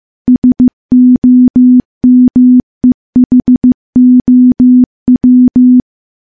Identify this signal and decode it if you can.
Morse code, heard as a keyed tone, about 15 wpm, 250 Hz. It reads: SOMEHOW
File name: morse-training-15WPM